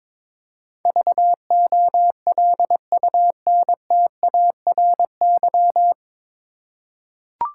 Morse code, heard as a keyed tone, about 22 wpm, 700 Hz. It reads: VOLUNTARY
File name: 9 Letter Words - First 3 Letters Given - 22wpm